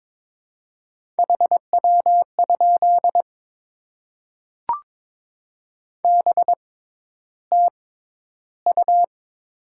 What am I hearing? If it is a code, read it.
Morse code, heard as a keyed tone, about 22 wpm, spaced as for 11 wpm, 700 Hz. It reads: HW? BTU
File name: Top 20 QSO Elements - 6x Character Spacing - 22wpm